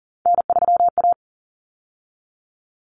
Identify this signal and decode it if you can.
Morse code, heard as a keyed tone, about 40 wpm, 700 Hz. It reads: N3A
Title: Common Cut Numbers - 3-Digit 40wpm